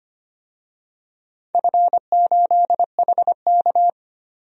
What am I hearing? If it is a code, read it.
Morse code, heard as a keyed tone, about 25 wpm, 700 Hz. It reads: F8HK
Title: DXCC-100 Call Signs - 4 Characters - 25wpm